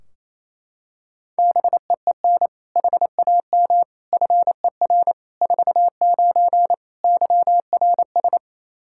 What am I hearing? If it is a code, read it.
Morse code, heard as a keyed tone, about 27 wpm, 700 Hz. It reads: BEEN HAM FER 49 YRS
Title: K1USN SST QSO Practice 28wpm